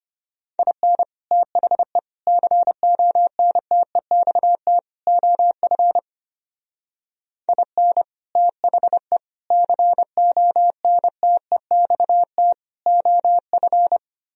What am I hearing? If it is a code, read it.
Morse code, heard as a keyed tone, about 31 wpm, 700 Hz. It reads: IN THE CONTEXT OF IN THE CONTEXT OF